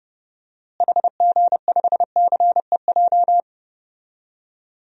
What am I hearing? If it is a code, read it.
Morse code, heard as a keyed tone, about 30 wpm, 700 Hz. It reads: HG5CEJ
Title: DXCC-100 Call Signs - 6 Characters - Repeated 3x - 30wpm